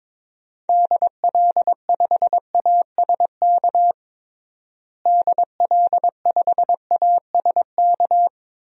Morse code, heard as a keyed tone, about 22 wpm, 700 Hz. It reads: DL5ASK DL5ASK
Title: DXCC-100 Call Signs - 6 Characters - Repeated 3x - 22wpm